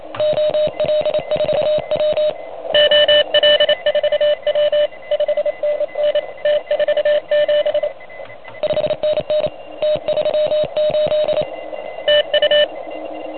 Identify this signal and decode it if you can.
Morse code, heard as a keyed tone, about 28 wpm, 590 Hz. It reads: OL4W OL4W 5NN T47 5NN T38 TU
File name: truSDX / MWC29.8